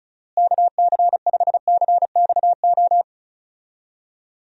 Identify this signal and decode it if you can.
Morse code, heard as a keyed tone, about 34 wpm, 700 Hz. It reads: KC5CXO
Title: USA Call Signs - 6 Characters - Repeated 2x - 35wpm